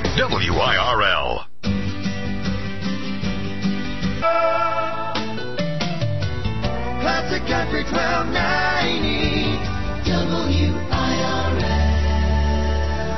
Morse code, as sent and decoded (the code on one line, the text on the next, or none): none